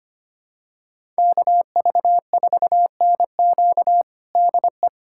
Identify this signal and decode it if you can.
Morse code, heard as a keyed tone, about 25 wpm, 700 Hz. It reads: KV4NQ DE
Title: USA Callsign DE Callsign 25wpm